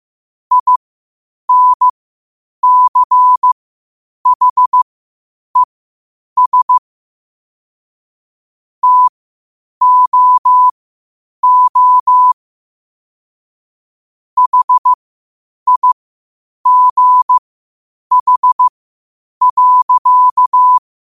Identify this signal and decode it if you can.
Morse code, heard as a keyed tone, about 15 wpm, spaced as for 8 wpm, 1000 Hz. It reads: INCHES TOO HIGH.
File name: qotdfiles / 8wpm